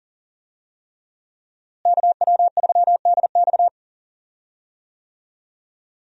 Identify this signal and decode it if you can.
Morse code, heard as a keyed tone, about 40 wpm, 700 Hz. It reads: KW3DX
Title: USA Call Signs - 5 Characters - Repeated 3x - 40wpm